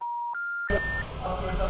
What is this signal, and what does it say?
Special information tone, with speech.
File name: out- -2003- - - .9292